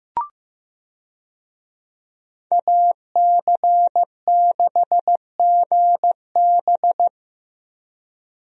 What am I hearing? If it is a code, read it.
Morse code, heard as a keyed tone, about 16 wpm, 700 Hz. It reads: AC6GB
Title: USA Call Signs - 5 Characters - 15wpm